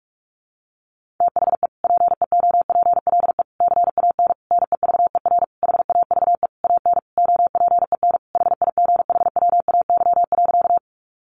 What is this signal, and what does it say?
Morse code, heard as a keyed tone, about 44 wpm, 700 Hz. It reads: THE PEOPLE CAN NEVER HAVE AN OPEN HIGHWAY.
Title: Sentences from Top 1400 Words 45wpm